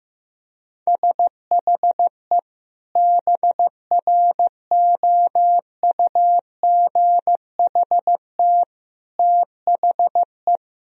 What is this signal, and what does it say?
Morse code, heard as a keyed tone, about 15 wpm, 700 Hz. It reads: SHE BROUGHT THE
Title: Sentences from Top 1000 Words - Review Part 4 - 15wpm